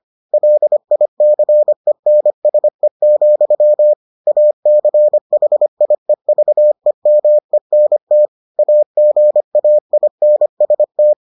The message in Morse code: LICENSE, ACHIEVEMENT AGAINST

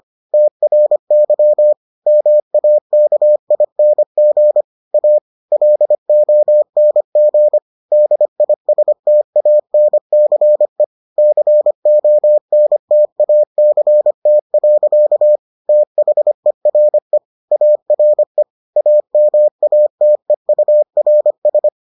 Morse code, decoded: TRY MAKING A LONG DISTANCE CONTACT. THERE ARE AMATEURS

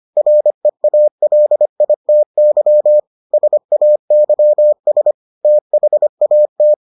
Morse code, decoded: REALITY SAYS THAT

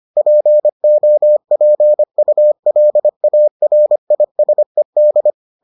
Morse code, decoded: POPULARISED

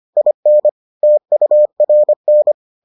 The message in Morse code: IN TURN

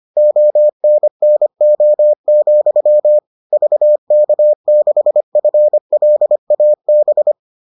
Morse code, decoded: ONNO, VK6FLAB

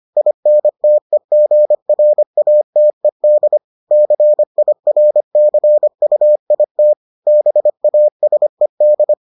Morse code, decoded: INTEGRATED CIRCUIT BASED

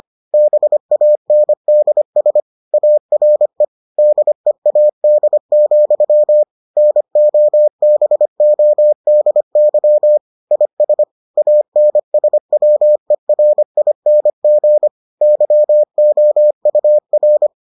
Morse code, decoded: BANDS ARE DEAD, NOBODY IS ANSWERING YOUR

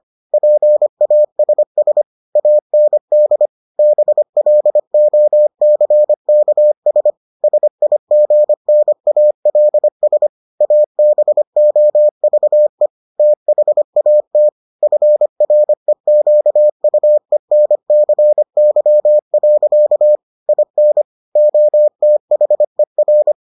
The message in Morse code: PASS AND BLOCKS SIGNALS ABOVE THAT FREQUENCY. IN OTHER

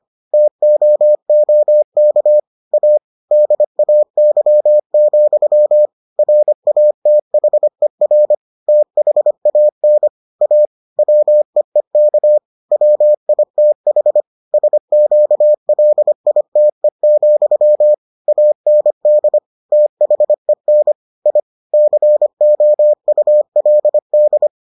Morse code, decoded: TOOK A DAY, RATHER THAN A WEEK WITH SQLITE, AND THEN I COULD

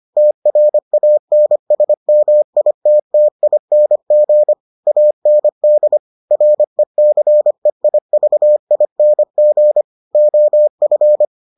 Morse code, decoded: TRANSMITTING AND RECEIVING OF

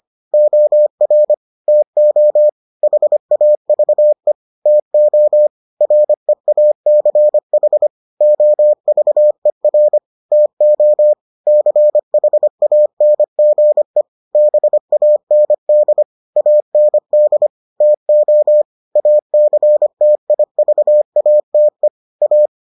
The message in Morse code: OR TO HAVE TO REACH OVER TO CHANGE BAND AND TO ACTIVATE A

